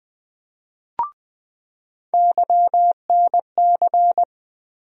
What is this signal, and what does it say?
Morse code, heard as a keyed tone, about 20 wpm, 700 Hz. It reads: YNC
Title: USA Call Sign Postfixes - 3 Letters 20wpm